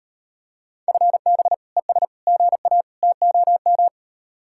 Morse code, heard as a keyed tone, about 38 wpm, 700 Hz. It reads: FB ES GA TOM